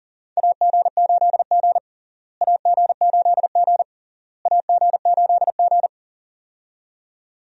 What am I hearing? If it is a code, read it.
Morse code, heard as a keyed tone, about 40 wpm, 700 Hz. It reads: AG8G AG8G AG8G
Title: USA Call Signs - 4 Characters - Repeated 3x - 40wpm